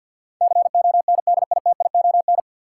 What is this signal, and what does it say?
Morse code, heard as a keyed tone, about 46 wpm, 700 Hz. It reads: CONDITION
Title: Sentences from Top 1000 Words - New Word Drill - 50wpm